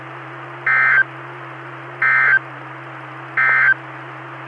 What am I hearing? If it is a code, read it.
Emergency Alert System (SAME) tones, heard as harsh buzzing bursts. It is heard from 0.5 to 3.8 s.
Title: Saline, AK SKYWARN Repeater System - W5RHS